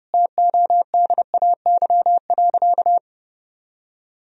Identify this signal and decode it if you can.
Morse code, heard as a keyed tone, about 30 wpm, 700 Hz. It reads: TODAY.
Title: Sentences from Top 1000 Words - Review Part 4 - 30wpm